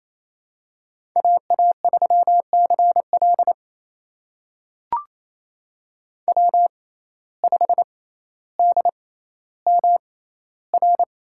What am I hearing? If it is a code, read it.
Morse code, heard as a keyed tone, about 28 wpm, spaced as for 14 wpm, 700 Hz. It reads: AA3CL W5DMR
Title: USA Call Signs - 5 Characters - 6x Character Spacing - 28wpm